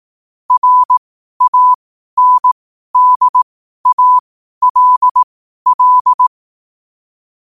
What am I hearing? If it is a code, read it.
Morse code, heard as a keyed tone, about 18 wpm, spaced as for 12 wpm, 1000 Hz. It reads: RANDALL
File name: qotdfiles / 12wpm